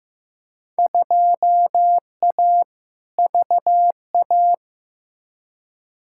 Morse code, heard as a keyed tone, about 15 wpm, 700 Hz. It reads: 2A VA
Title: Field Day Exchanges - 15wpm